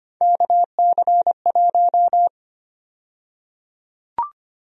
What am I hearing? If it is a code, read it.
Morse code, heard as a keyed tone, about 25 wpm, 700 Hz. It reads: KC1
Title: USA Call Sign Prefixes - Repeated 2x - 25wpm